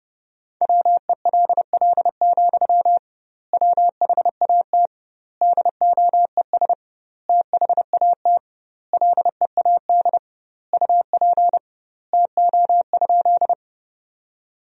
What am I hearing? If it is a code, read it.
Morse code, heard as a keyed tone, about 32 wpm, 700 Hz. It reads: WELL, WHAT DOES THAT LEAD UP TO?
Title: 2x Word-Spacing - Sentences from Top 500 Words - 30wpm